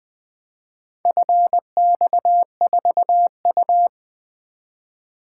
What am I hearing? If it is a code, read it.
Morse code, heard as a keyed tone, about 20 wpm, 700 Hz. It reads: FX4U
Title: Four Letter-Number Combinations - 20wpm